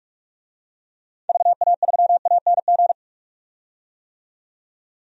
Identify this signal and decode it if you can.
Morse code, heard as a keyed tone, about 47 wpm, 700 Hz. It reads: VA3ANG
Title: DXCC-100 Call Signs - 6 Characters - 45wpm